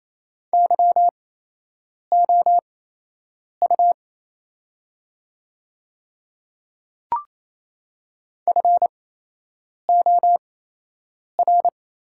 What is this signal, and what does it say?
Morse code, heard as a keyed tone, about 28 wpm, spaced as for 8 wpm, 700 Hz. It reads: YOU FOR
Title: 8x Character Spacing - Top 300 Words - 28wpm